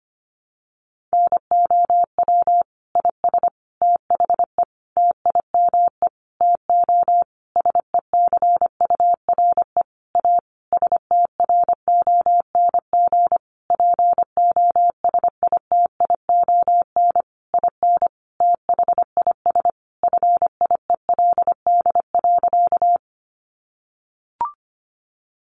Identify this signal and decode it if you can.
Morse code, heard as a keyed tone, about 25 wpm, 700 Hz. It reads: NOW IS THE TIME TO SECURE A STRONG POSITION IN THIS FIELD.
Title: Sentences from Top 1200 Words 25wpm